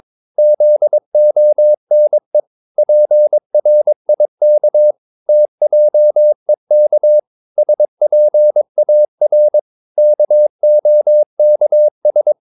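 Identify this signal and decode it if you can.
Morse code, heard as a keyed tone, about 22 wpm, 600 Hz. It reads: ZONE PRIK TJEK SPAR KOKS
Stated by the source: Morse practice of short Danish words at 22 wpm, from CW med Gnister.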